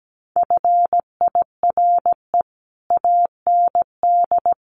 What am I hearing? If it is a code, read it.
Morse code, heard as a keyed tone, about 17 wpm, 700 Hz. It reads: FIRE AND
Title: Sentences from Top 400 Words 17wpm